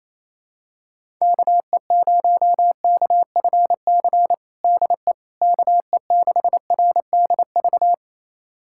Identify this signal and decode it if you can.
Morse code, heard as a keyed tone, about 28 wpm, 700 Hz. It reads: KE0KFC DE KE6RDV
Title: USA Callsign DE Callsign - 6 Characters - 28wpm